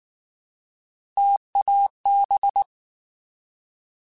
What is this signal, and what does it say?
Morse code, heard as a keyed tone, about 19 wpm, 800 Hz. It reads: TAB
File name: Alphabetical / Alphabetical mp3s / Ts @18